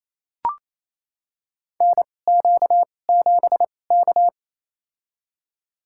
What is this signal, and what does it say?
Morse code, heard as a keyed tone, about 27 wpm, spaced as for 20 wpm, 700 Hz. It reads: NQ7K
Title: USA Call Signs - 2x Character Spacing - 28wpm